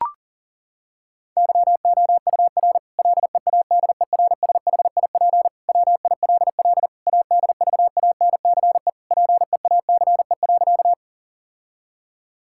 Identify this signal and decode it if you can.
Morse code, heard as a keyed tone, about 40 wpm, 700 Hz. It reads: YOUR LEADERSHIP WILL ADVANCE PEACE.
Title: Sentences from Top 1600 Words 40wpm